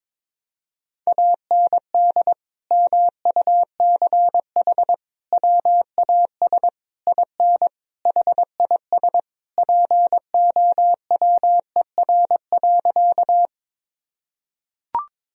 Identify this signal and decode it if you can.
Morse code, heard as a keyed tone, about 22 wpm, 700 Hz. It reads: AND MUCH WAS IN HIS POWER.
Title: Sentences from Top 1000 Words - Review Part 2 - 22wpm